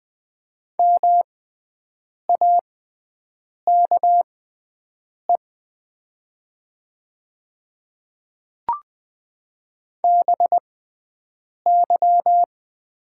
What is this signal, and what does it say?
Morse code, heard as a keyed tone, about 20 wpm, spaced as for 7 wpm, 700 Hz. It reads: MAKE BY
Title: 6x Character Spacing - Top 200 Words - 20wpm